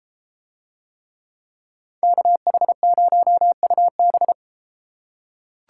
Morse code, heard as a keyed tone, about 33 wpm, 700 Hz. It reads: KH0UB